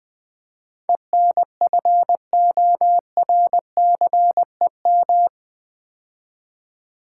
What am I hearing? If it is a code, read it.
Morse code, heard as a keyed tone, about 20 wpm, 700 Hz. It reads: ENFORCEM
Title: Build A Word - 4 Letters At A Time - 11-Letter Words - 20wpm